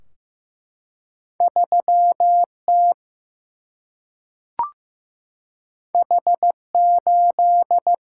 Morse code, heard as a keyed tone, about 15 wpm, 700 Hz. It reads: 3T H8
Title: Single-Letter Single-Number 15wpm